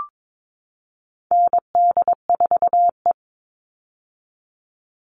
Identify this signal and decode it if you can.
Morse code, heard as a keyed tone, about 22 wpm, 700 Hz. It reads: ND4E
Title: All Cut Numbers - 4-Digit 22wpm